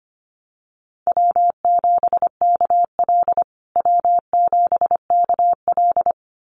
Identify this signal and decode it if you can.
Morse code, heard as a keyed tone, about 25 wpm, 700 Hz. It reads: W7KL W7KL
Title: USA Call Signs Repeated 2x 25wpm